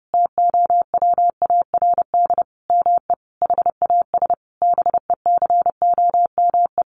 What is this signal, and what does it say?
Morse code, heard as a keyed tone, about 28 wpm, 700 Hz. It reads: TOWARD ME HAS BECOME
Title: Sentences from Top 1300 Words 30wpm